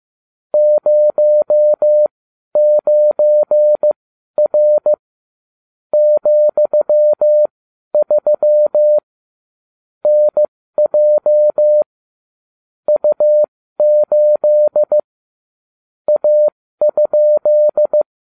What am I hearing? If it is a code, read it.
Morse code, heard as a keyed tone, about 15 wpm, spaced as for 12 wpm, 600 Hz. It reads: 09R ,3 NJ U8 A?